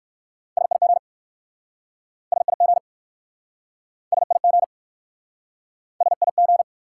Morse code, heard as a keyed tone, about 59 wpm, 700 Hz. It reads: SIG SIG SIG SIG